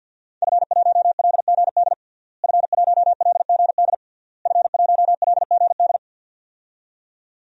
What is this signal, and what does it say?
Morse code, heard as a keyed tone, about 47 wpm, 700 Hz. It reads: F1LGD F1LGD F1LGD
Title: DXCC-100 Call Signs - 5 Characters - Repeated 3x - 50wpm